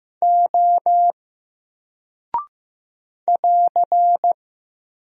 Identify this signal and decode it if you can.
Morse code, heard as a keyed tone, about 15 wpm, 700 Hz. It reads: O <AR>
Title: Single Character (Letter:Number:Punctuation:Prosign) 15wpm